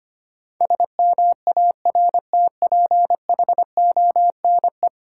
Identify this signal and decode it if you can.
Morse code, heard as a keyed tone, about 25 wpm, 700 Hz. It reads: SMARTPHONE
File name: Easy Words - Speed-Racing with Hints - 25wpm